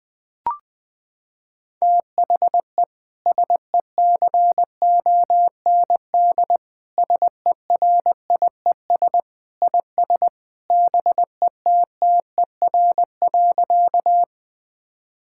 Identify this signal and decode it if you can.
Morse code, heard as a keyed tone, about 20 wpm, 700 Hz. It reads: THE SECOND SERIES IS BETTER.
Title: Sentences from Top 1000 Words - Review Part 5 - 20wpm